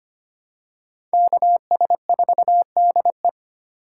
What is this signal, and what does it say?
Morse code, heard as a keyed tone, about 25 wpm, 700 Hz. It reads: KS4DE
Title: USA Call Signs - 5 Characters - 4x Character Spacing - 25wpm